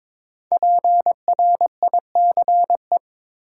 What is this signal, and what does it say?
Morse code, heard as a keyed tone, about 23 wpm, 700 Hz. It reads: PRICE
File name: 5 Letter Words - First 3 Letters Given - 22wpm